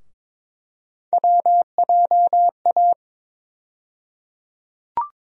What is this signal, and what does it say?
Morse code, heard as a keyed tone, about 22 wpm, 700 Hz. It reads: WJA
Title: USA Call Sign Postfixes - 3 Letters 22wpm